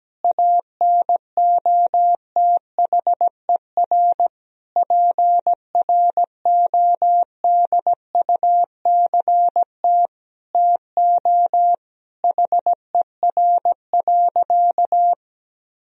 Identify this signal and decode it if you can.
Morse code, heard as a keyed tone, about 17 wpm, 700 Hz. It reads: ANOTHER PRODUCT TO HER.
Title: Sentences from Top 1000 Words - Review Part 3 - 17wpm